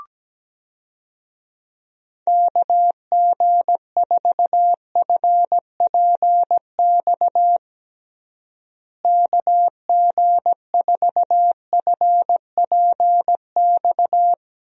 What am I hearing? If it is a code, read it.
Morse code, heard as a keyed tone, about 17 wpm, 700 Hz. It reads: KG4FPX KG4FPX
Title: USA Call Signs - 6 Characters - Repeated 3x - 17wpm